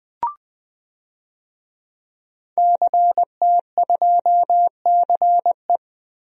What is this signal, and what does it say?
Morse code, heard as a keyed tone, about 21 wpm, 700 Hz. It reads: CT2CE
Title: DXCC-100 Call Signs - 5 Characters - Repeated 3x - 20wpm